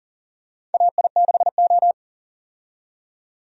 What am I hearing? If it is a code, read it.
Morse code, heard as a keyed tone, about 40 wpm, 700 Hz. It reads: AI6O
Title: USA Call Signs - 4 Characters - 40wpm